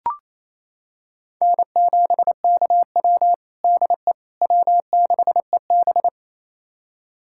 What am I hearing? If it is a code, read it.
Morse code, heard as a keyed tone, about 28 wpm, 700 Hz. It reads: N7KW DE W6EB
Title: USA Callsign DE Callsign - 4 Characters - 28wpm